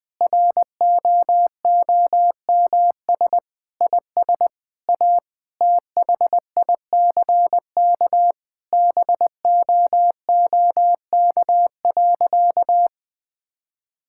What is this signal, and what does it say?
Morse code, heard as a keyed tone, about 20 wpm, 700 Hz. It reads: ROOMS IS A THICK BOOK.
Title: Sentences with 4-Word Phrases - 20wpm